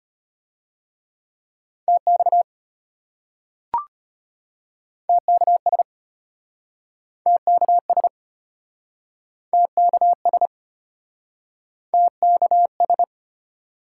Morse code, heard as a keyed tone, about 34 wpm, 700 Hz. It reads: TX TKS TKS TKS TKS